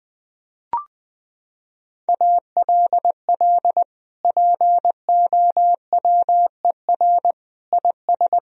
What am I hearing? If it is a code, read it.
Morse code, heard as a keyed tone, about 20 wpm, 700 Hz. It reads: ALL POWER IS
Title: Sentences from Top 1000 Words - Review Part 4 - 20wpm